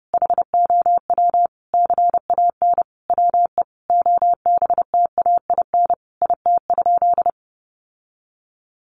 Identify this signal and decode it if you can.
Morse code, heard as a keyed tone, about 30 wpm, 700 Hz. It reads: HOW CAN WE OBTAIN IT?
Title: Sentences from Top 1300 Words 30wpm